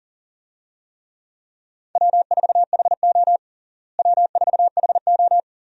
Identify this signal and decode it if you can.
Morse code, heard as a keyed tone, about 40 wpm, 700 Hz. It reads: W4HO W4HO
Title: USA Call Signs - 4 Characters - Repeated 2x - 40wpm